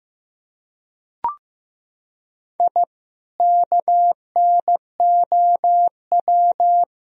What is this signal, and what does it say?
Morse code, heard as a keyed tone, about 15 wpm, 700 Hz. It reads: I KNOW
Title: Sentences from Top 1000 Words - Review Part 3 - 15wpm